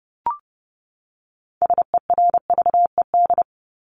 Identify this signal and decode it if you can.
Morse code, heard as a keyed tone, about 30 wpm, 700 Hz. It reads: SERVED
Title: ED Suffix - Top 100 Words with Base Words 30wpm